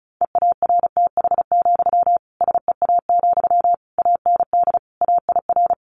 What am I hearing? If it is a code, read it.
Morse code, heard as a keyed tone, about 34 wpm, 700 Hz. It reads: EARTH, SEA, AND AIR
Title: Sentences from Top 1500 Words 35wpm